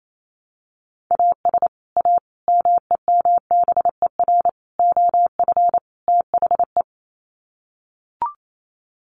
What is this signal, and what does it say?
Morse code, heard as a keyed tone, about 28 wpm, 700 Hz. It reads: AS A MEMBER OF THE
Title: Sets of 5 Words 28wpm